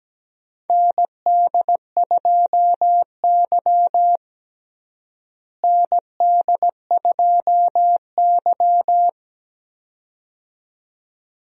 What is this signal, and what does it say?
Morse code, heard as a keyed tone, about 17 wpm, 700 Hz. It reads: ND2Y ND2Y
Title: USA Call Signs - 4 Characters - Repeated 2x - 17wpm